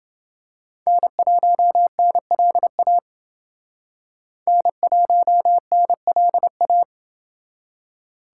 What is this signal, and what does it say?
Morse code, heard as a keyed tone, about 28 wpm, 700 Hz. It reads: N1NLA N1NLA